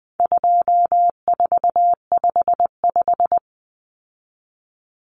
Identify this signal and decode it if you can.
Morse code, heard as a keyed tone, about 20 wpm, 700 Hz. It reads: 2455
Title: Common Cut Numbers - 4-Digit 20wpm